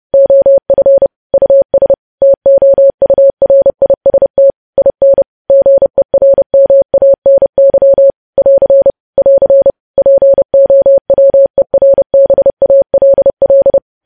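Morse code, heard as a keyed tone, about 30 wpm, 550 Hz. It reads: OF US TOURIST IN GERMANY <AR> <AR> POWERBALL
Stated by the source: News headlines sent in Morse at 30.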